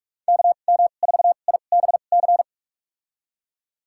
Morse code, heard as a keyed tone, about 41 wpm, spaced as for 31 wpm, 700 Hz. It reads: KM4IBC
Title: USA Call Signs - 6 Characters - 2x Character Spacing - 45wpm